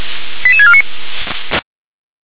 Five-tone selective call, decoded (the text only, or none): 88649